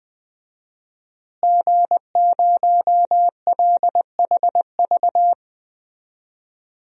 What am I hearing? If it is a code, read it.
Morse code, heard as a keyed tone, about 20 wpm, 700 Hz. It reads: G0LHV
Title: DXCC-100 Call Signs - 5 Characters - Repeated 3x - 20wpm